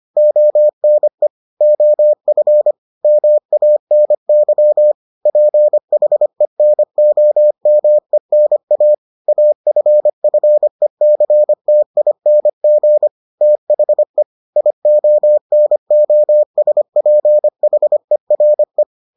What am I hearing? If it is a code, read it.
Morse code, heard as a keyed tone, about 25 wpm, 600 Hz. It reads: ONE OF MANY PHENOMENA AFFECTING THE IONOSPHERE